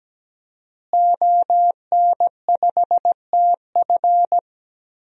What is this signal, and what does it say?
Morse code, heard as a keyed tone, about 17 wpm, 700 Hz. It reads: ON5TF
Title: DXCC-100 Call Signs - 5 Characters - Repeated 2x - 17wpm